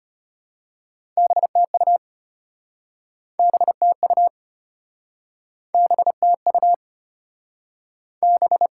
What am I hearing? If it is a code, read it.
Morse code, heard as a keyed tone, about 34 wpm, 700 Hz. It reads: BTU BTU BTU B